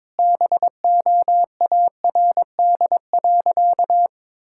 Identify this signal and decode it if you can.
Morse code, heard as a keyed tone, about 22 wpm, 700 Hz. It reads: BOARD.
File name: Sentences with 7-Word Phrases - 22wpm